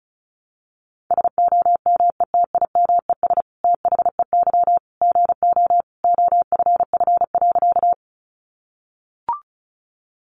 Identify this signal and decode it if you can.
Morse code, heard as a keyed tone, about 34 wpm, 700 Hz. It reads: SOMETIMES THEY GO OFF.
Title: Sentences from Top 600 Words 35wpm